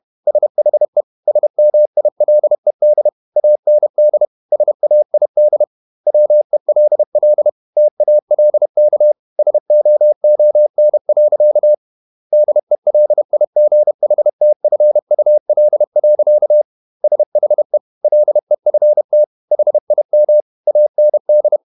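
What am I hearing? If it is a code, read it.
Morse code, heard as a keyed tone, about 31 wpm, 600 Hz. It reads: SHE SMILED AND SAID WELL TALK SOON. DELIGHTFUL. SHE LEFT HIM AND
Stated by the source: Morse code at 30 wpm, the novel Never by Ken Follett in English, from CW med Gnister.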